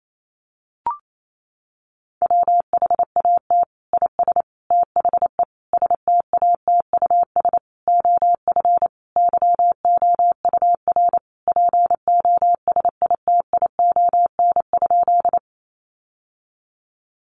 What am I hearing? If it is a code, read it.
Morse code, heard as a keyed tone, about 28 wpm, 700 Hz. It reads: WHAT IS THE STATUS OF YOUR POSITION?
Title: Sentences from Top 1400 Words 28wpm